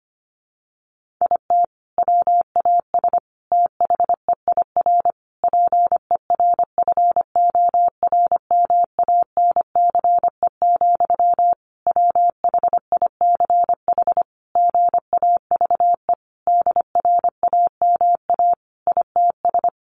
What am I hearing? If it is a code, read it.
Morse code, heard as a keyed tone, about 25 wpm, 700 Hz. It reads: IT WAS THEIR PERFORMANCE, WHICH GAVE DRAMA ITS
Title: Sentences from Top 1500 Words 25wpm